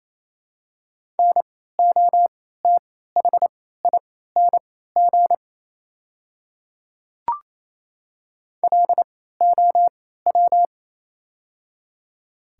Morse code, heard as a keyed tone, about 28 wpm, spaced as for 16 wpm, 700 Hz. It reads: NOTHING LOW
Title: 3x Character Spacing - Top 500 Words - 28wpm